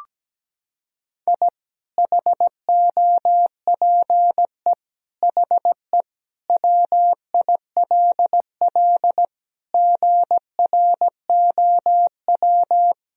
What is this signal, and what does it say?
Morse code, heard as a keyed tone, about 17 wpm, 700 Hz. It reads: I HOPE HE WILL GROW
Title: Sentences from Top 1000 Words - Review Part 1 - 17wpm